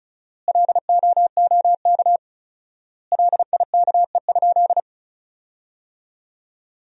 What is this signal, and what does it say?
Morse code, heard as a keyed tone, about 36 wpm, 700 Hz. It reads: LOOK LIKE?
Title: 4x Word-Spacing - Sentences from Top 100 Words - 35wpm